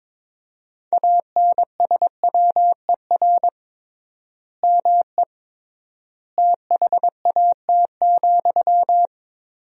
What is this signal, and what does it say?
Morse code, heard as a keyed tone, about 22 wpm, 700 Hz. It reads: ANSWER ME THAT,
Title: 3x Word-Spacing - Sentences from Top 500 Words - 22wpm